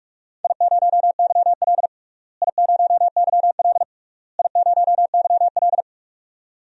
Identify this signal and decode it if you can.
Morse code, heard as a keyed tone, about 46 wpm, 700 Hz. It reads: I0YL I0YL I0YL
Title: DXCC-100 Call Signs - 4 Characters - Repeated 3x - 45wpm